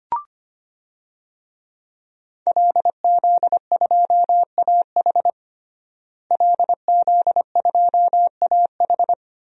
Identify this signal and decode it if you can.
Morse code, heard as a keyed tone, about 25 wpm, 700 Hz. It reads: LZ2AH LZ2AH
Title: DXCC-100 Call Signs - 5 Characters - Repeated 2x - 25wpm